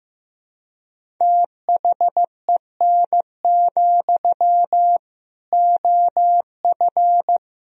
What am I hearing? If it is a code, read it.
Morse code, heard as a keyed tone, about 15 wpm, 700 Hz. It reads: THEN, OF
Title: Sentences from Top 1000 Words - Review Part 1 - 15wpm